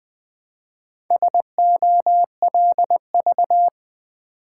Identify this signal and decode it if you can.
Morse code, heard as a keyed tone, about 20 wpm, 700 Hz. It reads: SOLV
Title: Build A Word - 1 Letter At A Time - 5-Letter Words - 20wpm